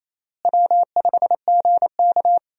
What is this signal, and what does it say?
Morse code, heard as a keyed tone, about 28 wpm, 700 Hz. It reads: W5GK
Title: USA Call Signs - 4 Characters - Repeated 3x - 28wpm